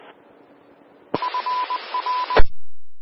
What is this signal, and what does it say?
Morse code, heard as a keyed tone, about 20 wpm, 1000 Hz. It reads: F R
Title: Austin, TX Amateur Repeaters